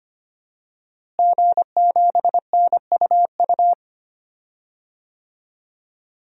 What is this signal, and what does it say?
Morse code, heard as a keyed tone, about 25 wpm, 700 Hz. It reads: G7NUU
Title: DXCC-100 Call Signs - 5 Characters - Repeated 2x - 25wpm